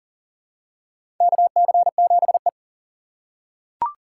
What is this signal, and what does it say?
Morse code, heard as a keyed tone, about 40 wpm, 700 Hz. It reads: KC7E
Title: USA Call Signs - 4x Character Spacing - 40wpm